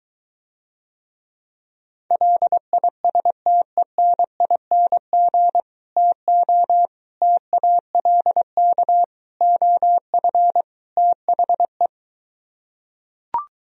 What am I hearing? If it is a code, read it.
Morse code, heard as a keyed tone, about 23 wpm, 700 Hz. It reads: LISTENING TO TALK OF THE